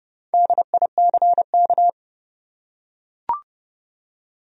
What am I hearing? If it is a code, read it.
Morse code, heard as a keyed tone, about 30 wpm, 700 Hz. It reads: DICK
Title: USA Common Names - Max 5 Letters - 30wpm